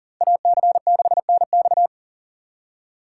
Morse code, heard as a keyed tone, about 49 wpm, 700 Hz. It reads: AC6NX
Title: Speed Racing - USA 2x2 Call Signs 40wpm